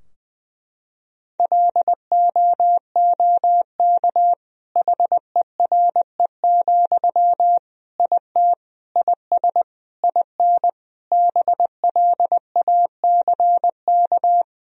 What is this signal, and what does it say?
Morse code, heard as a keyed tone, about 20 wpm, 700 Hz. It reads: LOOK HERE, IT IS IN BLACK
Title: Sentences from Top 1000 Words - Review Part 2 - 20wpm